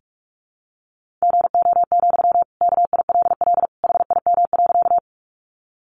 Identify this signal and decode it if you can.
Morse code, heard as a keyed tone, about 44 wpm, 700 Hz. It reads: GO, KILL HIM.
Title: Sentences from Top 1400 Words 45wpm